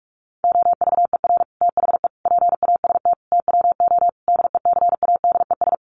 Morse code, heard as a keyed tone, about 44 wpm, 700 Hz. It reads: OVER THE PAST TWO DECADES
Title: Sets of 5 Words Encore2 45wpm